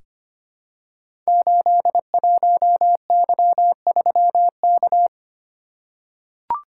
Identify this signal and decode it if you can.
Morse code, heard as a keyed tone, about 25 wpm, 700 Hz. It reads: 81Y3K
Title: Code Groups - 5 Letter:Numbers - 25wpm